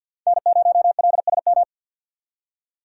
Morse code, heard as a keyed tone, about 52 wpm, 690 Hz. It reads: N0LSM
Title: USA Call Signs - 3x Character Spacing - 50wpm